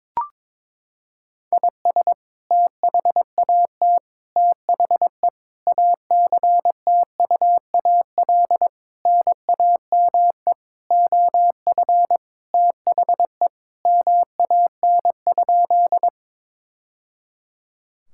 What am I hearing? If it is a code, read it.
Morse code, heard as a keyed tone, about 22 wpm, 700 Hz. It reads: IS THAT THE ACTUAL NAME OF THE MAN?
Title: Sentences from Top 1000 Words - Review Part 5 - 22wpm